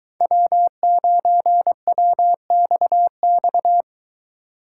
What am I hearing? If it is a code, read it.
Morse code, heard as a keyed tone, about 23 wpm, 700 Hz. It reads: W9WXX